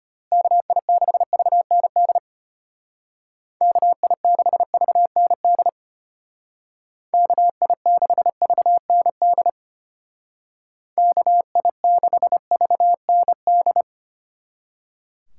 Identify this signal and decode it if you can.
Morse code, heard as a keyed tone, about 32 wpm, 700 Hz. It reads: KI6VND KI6VND KI6VND KI6VND